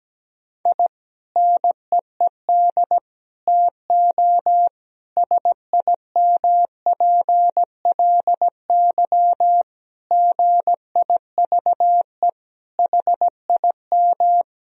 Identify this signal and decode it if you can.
Morse code, heard as a keyed tone, about 17 wpm, 700 Hz. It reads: I NEED TO SIMPLY GIVE HIM
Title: Sentences from Top 1000 Words - Review Part 4 - 17wpm